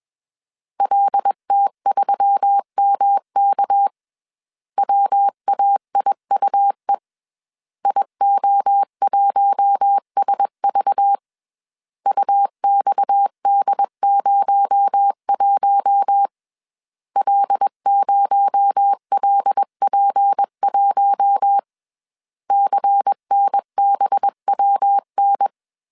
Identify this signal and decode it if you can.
Morse code, heard as a keyed tone, about 21 wpm, 800 Hz. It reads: LT3MK WAIUE IO1SV UXD01 L0LP1 CNBWN
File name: CW / DET / 1 6 1500C S C 20wpm 2116